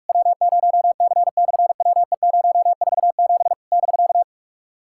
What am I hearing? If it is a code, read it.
Morse code, heard as a keyed tone, about 46 wpm, 700 Hz. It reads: W0C/WE047 <BK>
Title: SOTA QSO Practice 45wpm